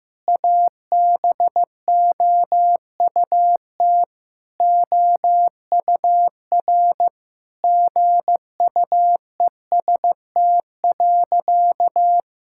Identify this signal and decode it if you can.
Morse code, heard as a keyed tone, about 15 wpm, 700 Hz. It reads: ABOUT OUR GUEST.
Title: Sentences from Top 1600 Words 15wpm